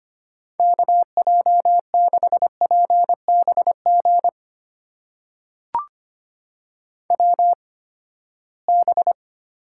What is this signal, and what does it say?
Morse code, heard as a keyed tone, about 25 wpm, spaced as for 17 wpm, 700 Hz. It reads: KJ6PBG WB
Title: USA Call Signs - 8x Character Spacing - 25wpm